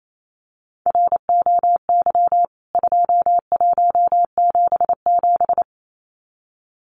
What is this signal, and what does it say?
Morse code, heard as a keyed tone, about 28 wpm, 700 Hz. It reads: ROY 2177
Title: CWT Exchange 28wpm